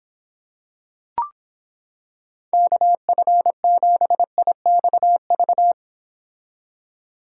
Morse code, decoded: KF7IXV

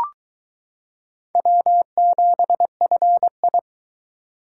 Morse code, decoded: W7FI

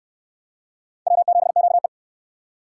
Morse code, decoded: W7QE

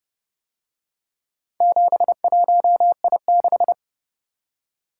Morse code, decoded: 71I6

